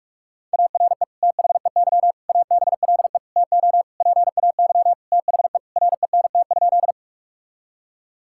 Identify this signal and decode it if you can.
Morse code, heard as a keyed tone, about 46 wpm, 700 Hz. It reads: ARE THEY ABLE TO PAY THE RENT?
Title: Sentences from Top 1000 Words - Review Part 5 - 45wpm